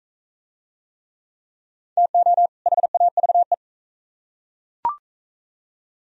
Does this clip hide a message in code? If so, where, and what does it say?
Yes: Morse code from 2.0 to 3.5 s, heard as a keyed tone, about 37 wpm, 700 Hz. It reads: TO HAVE